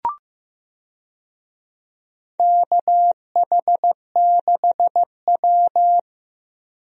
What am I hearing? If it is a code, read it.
Morse code, heard as a keyed tone, about 15 wpm, 700 Hz. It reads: KH6W
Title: DXCC-100 Call Signs - 4 Characters - 15wpm